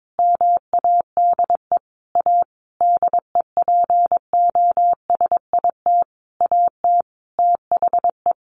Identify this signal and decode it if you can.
Morse code, heard as a keyed tone, about 22 wpm, 700 Hz. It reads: MADE A DEPOSIT AT THE
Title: Sentences from Top 1400 Words 22wpm